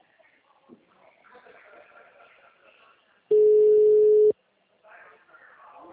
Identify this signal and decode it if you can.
Ringback tone (Europe, 425 Hz), with speech.